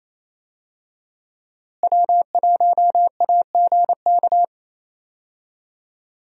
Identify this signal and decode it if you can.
Morse code, heard as a keyed tone, about 28 wpm, 700 Hz. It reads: W1AGK
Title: USA Call Signs - 5 Characters - Repeated 2x - 28wpm